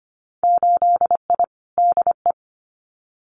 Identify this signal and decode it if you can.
Morse code, heard as a keyed tone, about 25 wpm, 700 Hz. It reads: 8I DE
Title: Winter Field Day Exchanges Encore 25wpm